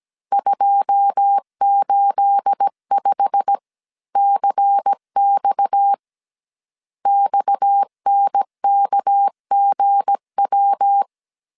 Morse code, heard as a keyed tone, about 17 wpm, 800 Hz. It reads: 285CX XNKGW
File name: CW / DET / 1 9 1500C S C 16wpm 2909